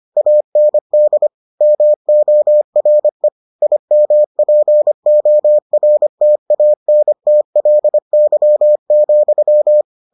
Morse code, heard as a keyed tone, about 25 wpm, 600 Hz. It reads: AND MORE IMPORTANTLY,